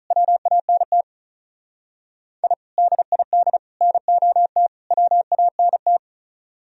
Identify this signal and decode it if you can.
Morse code, heard as a keyed tone, about 46 wpm, 700 Hz. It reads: WANT I DID NOT WANT